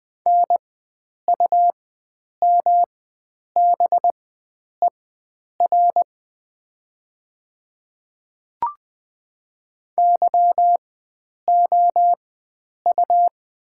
Morse code, heard as a keyed tone, about 20 wpm, spaced as for 9 wpm, 700 Hz. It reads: NUMBER YOU
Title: 4x Character Spacing - Top 100 Words - 20wpm